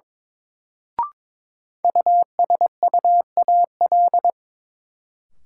Morse code, heard as a keyed tone, about 23 wpm, 700 Hz. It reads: USUAL
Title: Sentences from Top 400 Words - New Word Drill - 22wpm